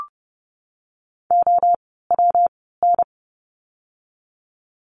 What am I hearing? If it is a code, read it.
Morse code, heard as a keyed tone, about 30 wpm, 700 Hz. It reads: O W N
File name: course / 30:10wpm / 115 Lesson 19f - Letter w - Words 30wpm